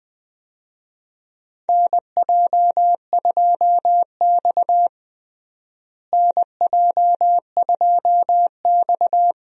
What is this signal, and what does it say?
Morse code, heard as a keyed tone, about 21 wpm, 700 Hz. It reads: NJ2X NJ2X
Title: USA Call Signs - 4 Characters - Repeated 3x - 20wpm